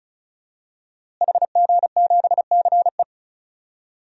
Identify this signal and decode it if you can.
Morse code, heard as a keyed tone, about 34 wpm, 700 Hz. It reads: HG7CE
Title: DXCC-100 Call Signs - 5 Characters - Repeated 3x - 35wpm